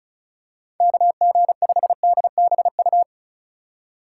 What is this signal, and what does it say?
Morse code, heard as a keyed tone, about 36 wpm, 700 Hz. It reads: KG5DBU
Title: USA Call Signs - 6 Characters - 3x Character Spacing - 35wpm